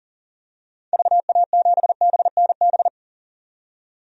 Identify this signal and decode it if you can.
Morse code, heard as a keyed tone, about 40 wpm, 700 Hz. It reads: VA7BNB
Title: DXCC-100 Call Signs - 6 Characters - 40wpm